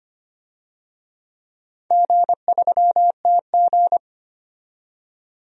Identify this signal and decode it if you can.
Morse code, heard as a keyed tone, about 25 wpm, 700 Hz. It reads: G3TG
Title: DXCC-100 Call Signs - 4 Characters - 25wpm